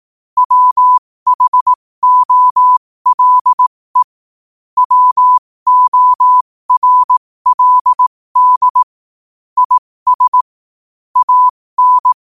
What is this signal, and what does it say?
Morse code, heard as a keyed tone, about 18 wpm, spaced as for 15 wpm, 1000 Hz. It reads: WHOLE WORLD IS AN
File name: qotdfiles / 15wpm